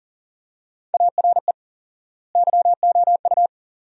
Morse code, heard as a keyed tone, about 37 wpm, 700 Hz. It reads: ARE YOU
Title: 4x Word-Spacing - Sentences from Top 500 Words - 40wpm